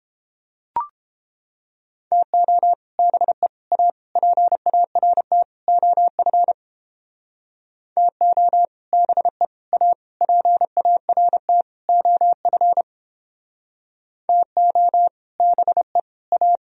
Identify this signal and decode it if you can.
Morse code, heard as a keyed tone, about 29 wpm, 700 Hz. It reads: TO BE A PART OF TO BE A PART OF TO BE A